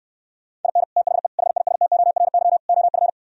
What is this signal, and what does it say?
Morse code, heard as a keyed tone, about 68 wpm, 700 Hz. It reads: IN THE HISTORY OF